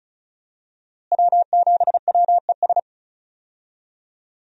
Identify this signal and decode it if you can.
Morse code, heard as a keyed tone, about 36 wpm, 700 Hz. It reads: W7WES
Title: USA Call Signs - 5 Characters - Repeated 2x - 35wpm